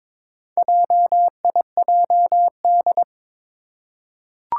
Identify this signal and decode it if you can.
Morse code, heard as a keyed tone, about 22 wpm, 700 Hz. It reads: JIJD
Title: Four Challenging Letter-Number Combinations 22wpm